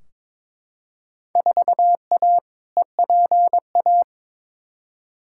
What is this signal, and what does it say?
Morse code, heard as a keyed tone, about 22 wpm, 700 Hz. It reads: 4A EPA
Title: Field Day Exchanges - 22wpm